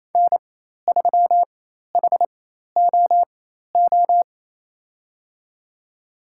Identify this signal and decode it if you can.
Morse code, heard as a keyed tone, about 28 wpm, spaced as for 13 wpm, 700 Hz. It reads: N3HOO
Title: USA Call Signs - 5 Characters - 4x Character Spacing - 28wpm